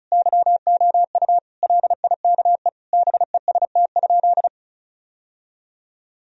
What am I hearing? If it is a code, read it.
Morse code, heard as a keyed tone, about 36 wpm, 700 Hz. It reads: YOU LIKE BEST?
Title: Sentences from Top 1000 Words - Review Part 1 - 35wpm